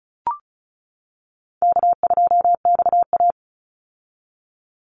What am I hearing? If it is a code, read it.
Morse code, heard as a keyed tone, about 34 wpm, 700 Hz. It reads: K2XA
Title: Call Signs - 35wpm